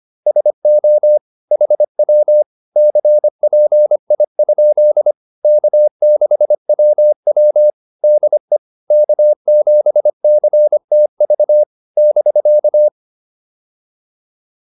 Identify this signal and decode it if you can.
Morse code, heard as a keyed tone, about 25 wpm, 600 Hz. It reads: SO HW CPI? K6WW DE K7CTV <BK>
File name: QSO204a 25